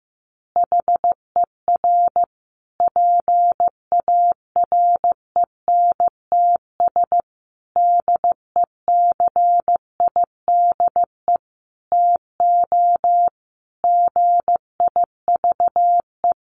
Morse code, heard as a keyed tone, about 15 wpm, 700 Hz. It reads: HER PARENTS DECIDE TO GIVE
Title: Easy Phrases 15wpm